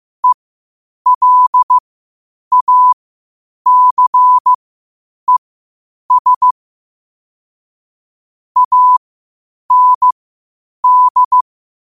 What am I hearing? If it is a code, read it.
Morse code, heard as a keyed tone, about 15 wpm, spaced as for 8 wpm, 1000 Hz. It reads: ELACES AND
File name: qotdfiles / 8wpm